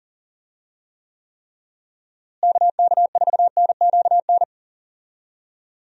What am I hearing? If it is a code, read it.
Morse code, heard as a keyed tone, about 40 wpm, 700 Hz. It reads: KK4NQN
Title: USA Call Signs - 6 Characters - Repeated 3x - 40wpm